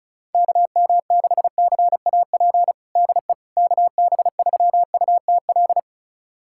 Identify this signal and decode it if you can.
Morse code, heard as a keyed tone, about 36 wpm, 700 Hz. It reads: KM6CAP DE KB3UTL
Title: USA Callsign DE Callsign 35wpm